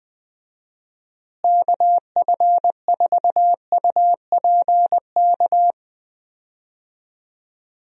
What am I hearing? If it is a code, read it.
Morse code, heard as a keyed tone, about 20 wpm, 700 Hz. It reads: KF4UPK
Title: USA Call Signs - 6 Characters - Repeated 3x - 20wpm